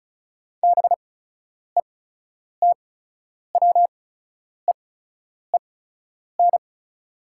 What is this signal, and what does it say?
Morse code, heard as a keyed tone, about 34 wpm, spaced as for 10 wpm, 700 Hz. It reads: BETWEEN
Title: 8x Character Spacing - Top 300 Words - 35wpm